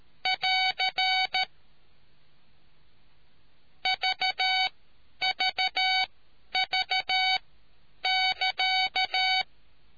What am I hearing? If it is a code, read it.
Morse code, heard as a keyed tone, about 13 wpm, spaced as for 9 wpm, 750 Hz. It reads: <AR> VVV<KA>